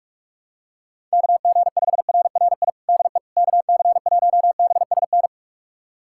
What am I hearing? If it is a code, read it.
Morse code, heard as a keyed tone, about 44 wpm, 700 Hz. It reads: KG5RRI DE KC1BSN
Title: USA Callsign DE Callsign 45wpm